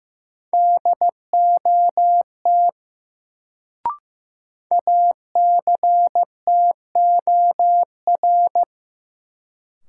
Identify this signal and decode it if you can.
Morse code, heard as a keyed tone, about 15 wpm, 700 Hz. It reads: DOT ACTOR
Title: Sentences from Top 1500 Words - New Word Drill - 15wpm